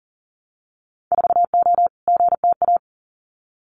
Morse code, heard as a keyed tone, about 40 wpm, 700 Hz. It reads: 4O GTA
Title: Winter Field Day Exchanges 40wpm